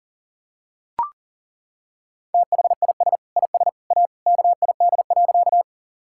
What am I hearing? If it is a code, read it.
Morse code, heard as a keyed tone, about 37 wpm, 700 Hz. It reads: THIS IS A KID.
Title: Sentences from Top 1600 Words 40wpm